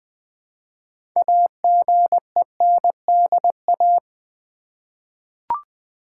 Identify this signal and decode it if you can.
Morse code, heard as a keyed tone, about 20 wpm, 700 Hz. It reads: AGENDA
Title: 6 Letter Words - First 3 Letters Given - 20wpm